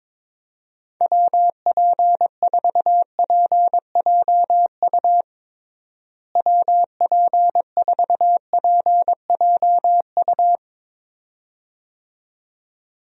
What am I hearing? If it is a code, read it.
Morse code, heard as a keyed tone, about 22 wpm, 700 Hz. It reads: WP4PJU WP4PJU
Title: USA Call Signs - 6 Characters - Repeated 2x - 22wpm